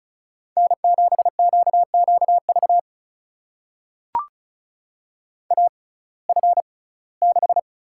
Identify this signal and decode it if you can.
Morse code, heard as a keyed tone, about 34 wpm, spaced as for 21 wpm, 700 Hz. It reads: N7QQV AF6
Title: USA Call Signs - 5 Characters - 6x Character Spacing - 35wpm